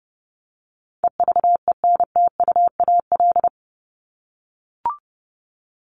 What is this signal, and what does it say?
Morse code, heard as a keyed tone, about 30 wpm, 700 Hz. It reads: EVENTUAL
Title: LY Suffix - Top 100 Words with Base Words 30wpm